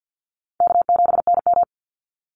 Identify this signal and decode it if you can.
Morse code, heard as a keyed tone, about 50 wpm, 700 Hz. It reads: K7NM
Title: Easy Call Signs - 50wpm